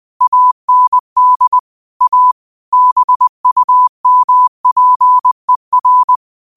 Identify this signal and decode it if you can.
Morse code, heard as a keyed tone, about 20 wpm, 1000 Hz. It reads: AND A BUMPER